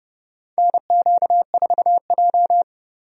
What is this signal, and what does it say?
Morse code, heard as a keyed tone, about 30 wpm, 700 Hz. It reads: NQ4J
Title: USA Call Signs - 4 Characters - Repeated 3x - 30wpm